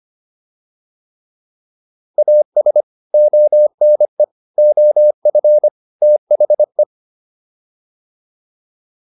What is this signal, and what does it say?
Morse code, heard as a keyed tone, about 25 wpm, 600 Hz. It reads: AS ONE OF THE